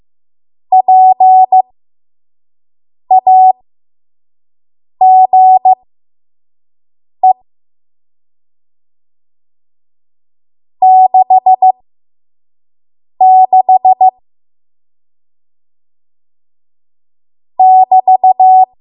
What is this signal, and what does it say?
Morse code, heard as a keyed tone, about 15 wpm, spaced as for 5 wpm, 750 Hz. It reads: PAGE 66 <BT>
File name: Morse / 05WPM